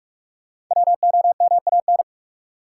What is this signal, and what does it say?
Morse code, heard as a keyed tone, about 46 wpm, 700 Hz. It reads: WOMAN
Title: Sentences from Top 1000 Words - Review Part 2 - New Word Drill - 45wpm